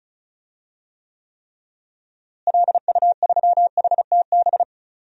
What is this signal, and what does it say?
Morse code, heard as a keyed tone, about 36 wpm, 700 Hz. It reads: LU3HTB
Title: DXCC-100 Call Signs - 6 Characters - Repeated 3x - 35wpm